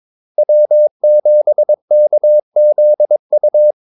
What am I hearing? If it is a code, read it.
Morse code, heard as a keyed tone, about 22 wpm, 600 Hz. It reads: W7KZU
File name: Callsigns-22wpm